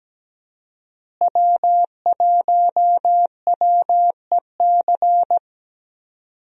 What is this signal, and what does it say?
Morse code, heard as a keyed tone, about 17 wpm, 700 Hz. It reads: W1WEC
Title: USA Call Signs - 5 Characters - Repeated 3x - 17wpm